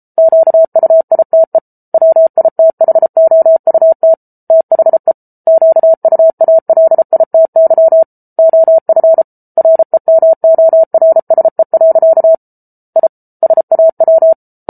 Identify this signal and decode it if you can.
Morse code, heard as a keyed tone, about 34 wpm, 650 Hz. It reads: QUITE WITHOUT THE QUALITY OF REMORSE. I SAW